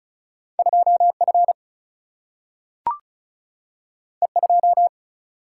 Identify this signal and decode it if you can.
Morse code, heard as a keyed tone, about 34 wpm, 700 Hz. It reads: 2F E2
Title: Single-Letter Single-Number 35wpm